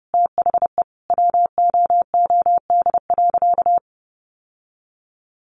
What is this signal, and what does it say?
Morse code, heard as a keyed tone, about 30 wpm, 700 Hz. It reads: THE WOOD.
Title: Sentences from Top 1200 Words 30wpm